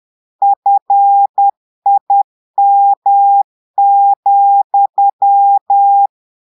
Morse code, heard as a keyed tone, about 10 wpm, 800 Hz. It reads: FIM,
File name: portugais texte en clair 2 vitesse 10 mots minutes